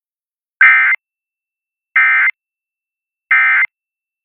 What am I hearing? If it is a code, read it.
Emergency Alert System (SAME) tones, heard as harsh buzzing bursts. It is heard from 0.6 to 3.6 s.